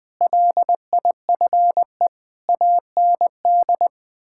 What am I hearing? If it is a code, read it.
Morse code, heard as a keyed tone, about 20 wpm, 700 Hz. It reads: LIFE AND
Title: Sentences with 8-Word Phrases - 20wpm